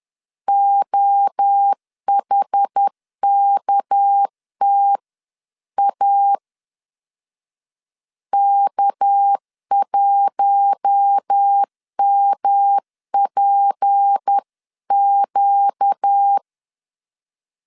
Morse code, heard as a keyed tone, about 10 wpm, 800 Hz. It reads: OHKTA K1MPQ
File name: LEZIONI / 6 PKX0 / 6 PK 10wpm 2436